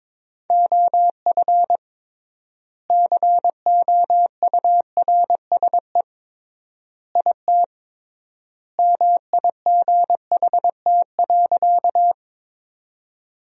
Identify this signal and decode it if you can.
Morse code, heard as a keyed tone, about 22 wpm, 700 Hz. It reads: OF COURSE IT MIGHT.
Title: 3x Word-Spacing - Sentences from Top 500 Words - 22wpm